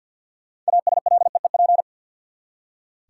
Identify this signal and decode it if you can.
Morse code, heard as a keyed tone, about 47 wpm, 700 Hz. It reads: ASLEEP
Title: Six Letter Words - 50wpm